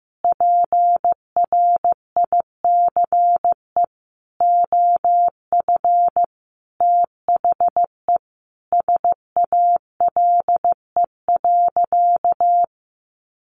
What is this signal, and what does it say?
Morse code, heard as a keyed tone, about 15 wpm, 700 Hz. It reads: PRICE OF THE SALE.
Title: Sentences from Top 1400 Words 15wpm